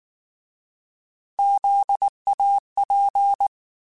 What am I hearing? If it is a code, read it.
Morse code, heard as a keyed tone, about 19 wpm, 800 Hz. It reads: ZAP
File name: Alphabetical / Alphabetical mp3s / Zs @18